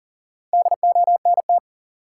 Morse code, heard as a keyed tone, about 40 wpm, 700 Hz. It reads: DONT
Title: Top 180 QSO Elements 40wpm